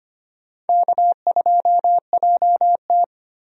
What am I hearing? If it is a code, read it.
Morse code, heard as a keyed tone, about 25 wpm, 700 Hz. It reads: K2JT
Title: USA Call Signs - 4 Characters - Repeated 3x - 25wpm